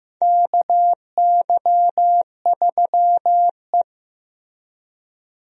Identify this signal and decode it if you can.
Morse code, heard as a keyed tone, about 15 wpm, 700 Hz. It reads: KY3E
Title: USA Call Signs - 4 Characters - Repeated 2x - 15wpm